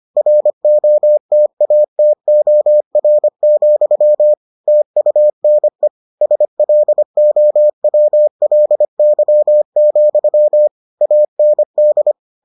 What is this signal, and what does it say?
Morse code, heard as a keyed tone, about 25 wpm, 600 Hz. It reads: ROTATOR, TUNE SLOWLY, AND